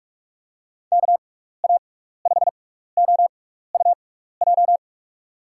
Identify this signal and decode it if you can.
Morse code, heard as a keyed tone, about 46 wpm, spaced as for 15 wpm, 700 Hz. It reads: KA5OUJ
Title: USA Call Signs - 6x Character Spacing - 45wpm